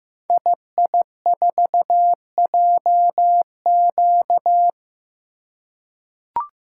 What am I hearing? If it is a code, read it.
Morse code, heard as a keyed tone, about 15 wpm, 700 Hz. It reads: II4JQ
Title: Code Groups - 5 Letter:Numbers - 15wpm